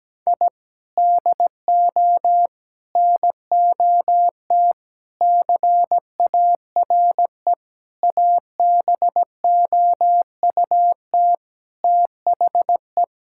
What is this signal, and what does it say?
Morse code, heard as a keyed tone, about 17 wpm, 700 Hz. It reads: I DO NOT CARE ABOUT THE
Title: Sentences from Top 1000 Words - Review Part 5 - 17wpm